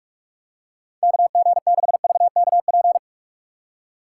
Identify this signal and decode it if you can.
Morse code, heard as a keyed tone, about 42 wpm, 700 Hz. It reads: KG6VKP
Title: USA Call Signs - 6 Characters - 6x Character Spacing - 45wpm